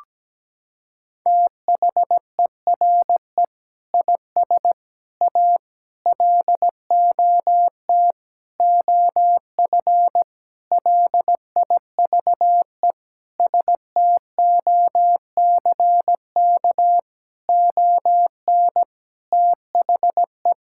Morse code, heard as a keyed tone, about 17 wpm, 700 Hz. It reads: THERE IS A LOT OF LIVE STOCK ON THE
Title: Sentences from Top 1000 Words - Review Part 5 - 17wpm